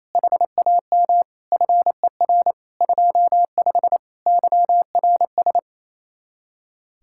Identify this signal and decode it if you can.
Morse code, heard as a keyed tone, about 34 wpm, 700 Hz. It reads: HAM FER 25 YRS
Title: Speed-Racing Common QSO Phrases - 28wpm